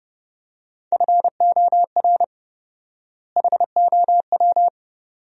Morse code, heard as a keyed tone, about 32 wpm, 700 Hz. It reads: FOR HOW
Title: 4x Word-Spacing - Sentences from Top 200 Words - 30wpm